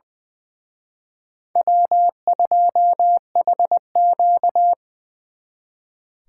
Morse code, heard as a keyed tone, about 24 wpm, 700 Hz. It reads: W2HQ
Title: Speed Racing - USA 1x2 Call Signs 20wpm